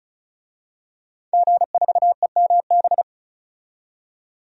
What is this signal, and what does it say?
Morse code, heard as a keyed tone, about 34 wpm, 700 Hz. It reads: G4EMB
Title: DXCC-100 Call Signs - 5 Characters - 35wpm